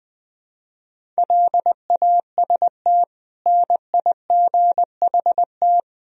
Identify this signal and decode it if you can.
Morse code, heard as a keyed tone, about 20 wpm, 700 Hz. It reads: LAST NIGHT
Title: Sentences from Top 1000 Words - Review Part 4 - 20wpm